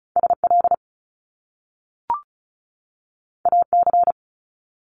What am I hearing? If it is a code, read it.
Morse code, heard as a keyed tone, about 36 wpm, 700 Hz. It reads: SL AC
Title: Top Two Letter Combinations in Words 35wpm